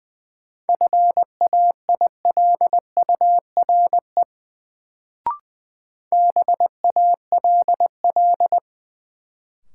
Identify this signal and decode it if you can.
Morse code, heard as a keyed tone, about 20 wpm, 700 Hz. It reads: FAILURE BALL
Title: Sentences from Top 900 Words - New Word Drill - 20wpm